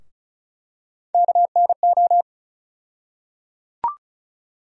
Morse code, heard as a keyed tone, about 36 wpm, 700 Hz. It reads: KNO
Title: USA Call Sign Postfixes - 3 Letters 35wpm